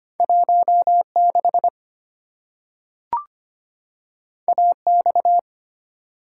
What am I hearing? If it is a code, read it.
Morse code, heard as a keyed tone, about 25 wpm, 700 Hz. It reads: 1 6 A X
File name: Two Letter-Number Combinations - 25wpm